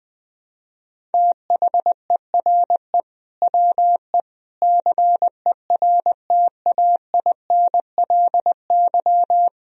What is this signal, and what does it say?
Morse code, heard as a keyed tone, about 21 wpm, 700 Hz. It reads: THERE WE CERTAINLY
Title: Sentences from Top 1000 Words - Review Part 1 - 20wpm